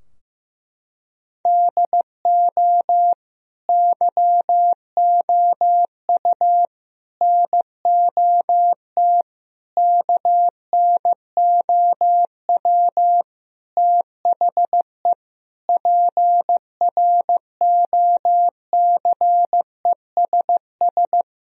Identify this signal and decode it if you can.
Morse code, heard as a keyed tone, about 16 wpm, 700 Hz. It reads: DO YOU NOT KNOW THE PROCESS
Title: Sentences from Top 1000 Words - Review Part 3 - 15wpm